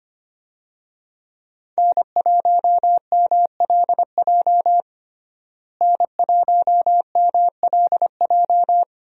Morse code, heard as a keyed tone, about 25 wpm, 700 Hz. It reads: N1MLJ N1MLJ
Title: USA Call Signs - 5 Characters - Repeated 3x - 25wpm